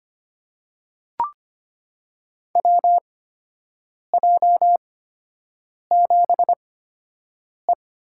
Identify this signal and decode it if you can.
Morse code, heard as a keyed tone, about 25 wpm, spaced as for 7 wpm, 700 Hz. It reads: WJ7E
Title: USA Call Signs - 4 Characters - 8x Character Spacing - 25wpm